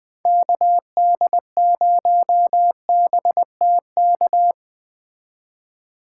Morse code, heard as a keyed tone, about 20 wpm, 700 Hz. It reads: KD0BTK
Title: Speed Racing - USA 2x3 Call Signs 17wpm